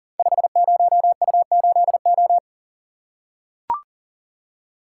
Morse code, heard as a keyed tone, about 37 wpm, 700 Hz. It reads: 50U8O
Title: Code Groups - 5 Letter:Numbers - 40wpm